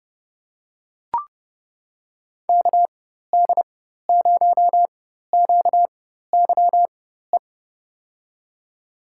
Morse code, heard as a keyed tone, about 30 wpm, spaced as for 14 wpm, 700 Hz. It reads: KD0QYE
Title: USA Call Signs - 6 Characters - 4x Character Spacing - 30wpm